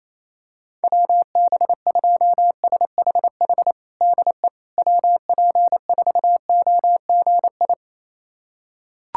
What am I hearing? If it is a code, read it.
Morse code, heard as a keyed tone, about 28 wpm, 700 Hz. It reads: WB2SHH DE WP4OGI
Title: USA Callsign DE Callsign - 6 Characters - 28wpm